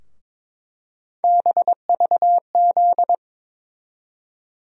Morse code, heard as a keyed tone, about 22 wpm, 700 Hz. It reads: BVZ
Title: USA Call Sign Postfixes - 3 Letters 22wpm